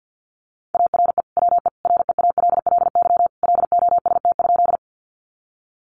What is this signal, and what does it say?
Morse code, heard as a keyed tone, about 50 wpm, 700 Hz. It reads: ARE WE REALLY LOST?
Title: Sentences from Top 1400 Words 50wpm